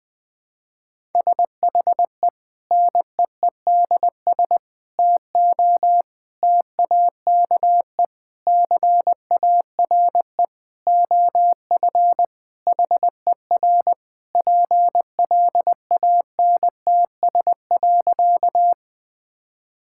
Morse code, heard as a keyed tone, about 20 wpm, 700 Hz. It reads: SHE NEEDS TO TAKE CARE OF HER PLANTS.
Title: Sentences with 4-Word Phrases - Spoken Beginning - 20wpm